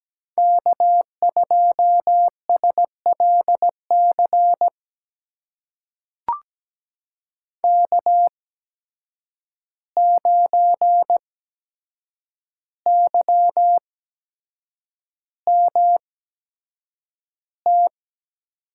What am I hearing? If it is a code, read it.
Morse code, heard as a keyed tone, about 17 wpm, spaced as for 7 wpm, 700 Hz. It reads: K2SLC K9YMT
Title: USA Call Signs - 5 Characters - 8x Character Spacing - 17wpm